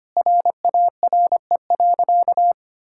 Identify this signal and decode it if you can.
Morse code, heard as a keyed tone, about 25 wpm, 700 Hz. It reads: RARE.
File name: practice / 25wpm / Sentences with 2-Word Phrases - 25wpm